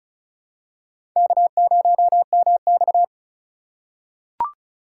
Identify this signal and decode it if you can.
Morse code, heard as a keyed tone, about 36 wpm, 700 Hz. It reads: K0MX
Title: USA Call Signs - 4 Characters - 4x Character Spacing - 35wpm